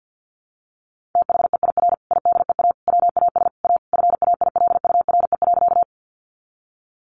Morse code, heard as a keyed tone, about 51 wpm, 700 Hz. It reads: THEIR IDEA WAS A FAILURE.
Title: Sentences from Top 900 Words 50wpm